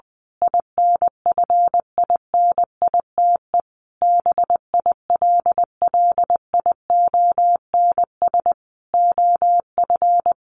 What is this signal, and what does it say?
Morse code, heard as a keyed tone, about 20 wpm, 700 Hz. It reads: INFINITE BILLIONS OF
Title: A Sound Of Thunder 20wpm